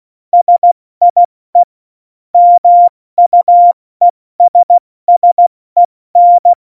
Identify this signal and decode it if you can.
Morse code, heard as a keyed tone, about 16 wpm, 700 Hz. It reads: SIE MUESSEN
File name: Morsekurs / Witze / Witz 17. November 2020, 70 F80